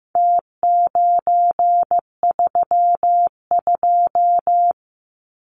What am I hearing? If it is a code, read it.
Morse code, heard as a keyed tone, about 15 wpm, 700 Hz. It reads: T932
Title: Numbers - Contest Style - 4-Digit 15wpm